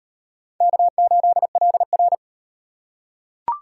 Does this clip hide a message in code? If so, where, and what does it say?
Yes: Morse code from 0.6 to 2.1 s, heard as a keyed tone, about 36 wpm, 700 Hz. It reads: K8LR